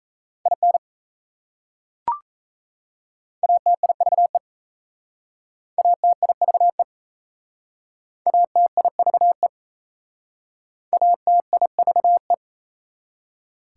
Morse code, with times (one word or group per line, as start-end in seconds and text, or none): 0.5-0.8 IN
3.4-4.4 ATIVE
5.8-6.8 ATIVE
8.3-9.5 ATIVE
10.9-12.3 ATIVE